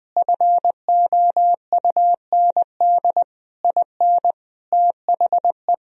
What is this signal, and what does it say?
Morse code, heard as a keyed tone, about 21 wpm, 700 Hz. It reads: FOUND IN THE
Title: Sentences from Top 1000 Words - Review Part 1 - 20wpm